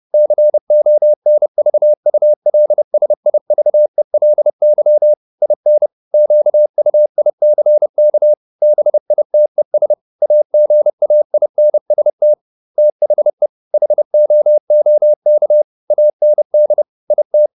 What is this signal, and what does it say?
Morse code, heard as a keyed tone, about 30 wpm, 600 Hz. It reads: CONVULSIVELY IN QUICK BITES AGAINST THE HOOK AND IT